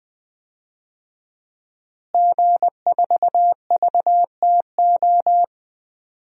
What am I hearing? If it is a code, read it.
Morse code, heard as a keyed tone, about 20 wpm, 700 Hz. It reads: G4VTO
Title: DXCC-100 Call Signs - 5 Characters - Repeated 3x - 20wpm